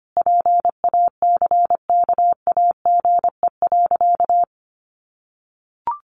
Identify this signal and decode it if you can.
Morse code, heard as a keyed tone, about 25 wpm, 700 Hz. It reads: PACKAGE.
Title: Additional Easy Phrases 25wpm